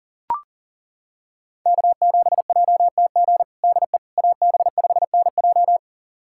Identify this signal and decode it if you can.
Morse code, heard as a keyed tone, about 40 wpm, 700 Hz. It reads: K7JTG DE AB5NJ
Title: USA Callsign DE Callsign - 5 Characters - 40wpm